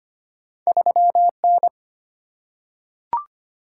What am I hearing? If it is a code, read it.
Morse code, heard as a keyed tone, about 25 wpm, 700 Hz. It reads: 3N
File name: Two Letter-Number Combinations - 25wpm